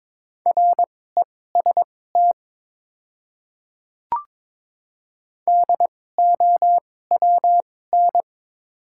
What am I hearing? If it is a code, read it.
Morse code, heard as a keyed tone, about 22 wpm, spaced as for 16 wpm, 690 Hz. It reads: REST DOWN
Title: 2x Character Spacing - Top 500 Words - 22wpm